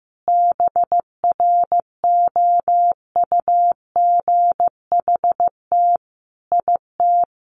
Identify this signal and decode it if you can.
Morse code, heard as a keyed tone, about 15 wpm, 700 Hz. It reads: BROUGHT IT
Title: Sentences from Top 500 Words 15wpm